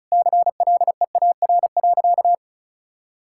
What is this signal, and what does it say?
Morse code, heard as a keyed tone, about 34 wpm, 700 Hz. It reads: CLEAR.
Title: Sentences from Top 1000 Words - Review Part 6 - 35wpm